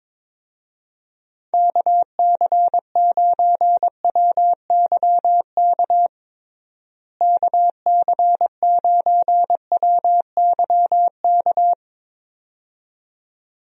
Morse code, heard as a keyed tone, about 22 wpm, 700 Hz. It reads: KC9WYK KC9WYK
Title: USA Call Signs - 6 Characters - Repeated 2x - 22wpm